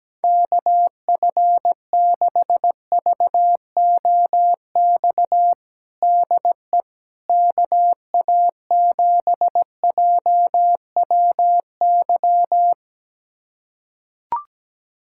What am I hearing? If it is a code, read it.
Morse code, heard as a keyed tone, about 17 wpm, 700 Hz. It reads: KF6VOX DE KA7JWY
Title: USA Callsign DE Callsign - 6 Characters - 17wpm